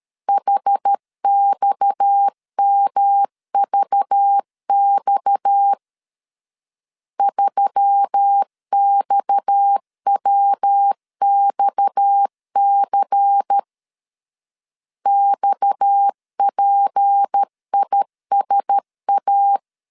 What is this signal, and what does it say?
Morse code, heard as a keyed tone, about 13 wpm, 800 Hz. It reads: HXMVX 3XWXC XPISA
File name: LEZIONI / 6 PKX0 / 6 PKX 12wpm 2024